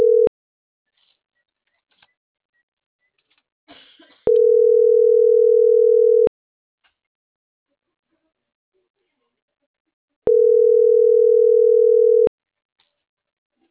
Ringback tone (North America), 440 + 480 Hz.